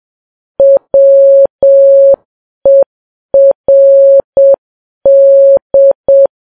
Morse code, heard as a keyed tone, about 7 wpm, 550 Hz. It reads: WERD